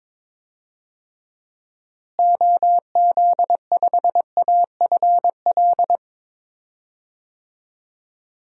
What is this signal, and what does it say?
Morse code, heard as a keyed tone, about 22 wpm, 700 Hz. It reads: OZ5AFL
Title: DXCC-100 Call Signs - 6 Characters - Repeated 3x - 22wpm